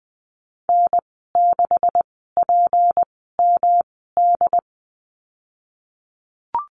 Morse code, heard as a keyed tone, about 20 wpm, 700 Hz. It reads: N 6 P M D
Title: 154 Lesson 25g - Letter p - Call Signs 20wpm